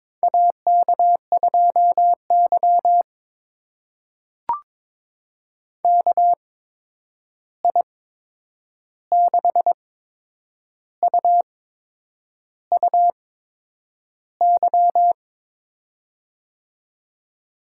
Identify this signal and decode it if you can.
Morse code, heard as a keyed tone, about 22 wpm, spaced as for 8 wpm, 700 Hz. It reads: AK2Y KI6UUY
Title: USA Call Signs - 8x Character Spacing - 22wpm